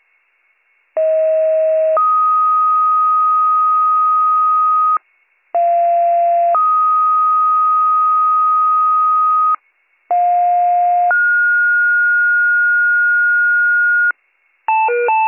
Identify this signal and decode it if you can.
Two-tone page, heard as a one-second tone, then a three-second tone, about 640 then 1250 Hz, about 690 then 1250 Hz, about 710 then 1430 Hz. It is heard from 1.0 to 14.1 s.